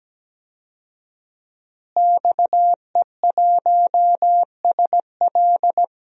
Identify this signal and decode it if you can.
Morse code, heard as a keyed tone, about 17 wpm, 700 Hz. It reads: XE1SL
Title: DXCC-100 Call Signs - 5 Characters - Repeated 3x - 17wpm